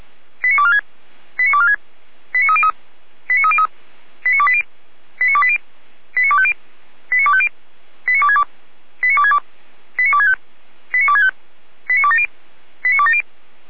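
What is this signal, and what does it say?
Five-tone selective call (ZVEI), heard as quick runs of five tones, tones of about 70 ms. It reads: 89246 89246 89303 89303 89280 89280 89250 89250 89262 89262 89265 89265 89270 89270
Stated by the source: ZVEI five-tone selective call 89229 received on an SDR.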